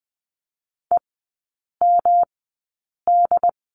Morse code, heard as a keyed tone, about 20 wpm, spaced as for 8 wpm, 700 Hz. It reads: EMD
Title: 178 Lesson 29e - Slash - Three Characters 20wpm